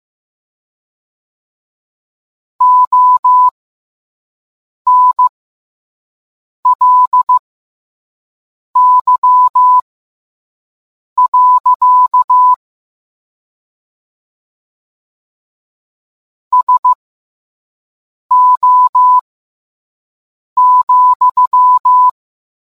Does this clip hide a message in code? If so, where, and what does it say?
Yes: Morse code from 2.6 to 22.1 s, heard as a keyed tone, about 15 wpm, spaced as for 5 wpm, 1000 Hz. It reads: ONLY. SO,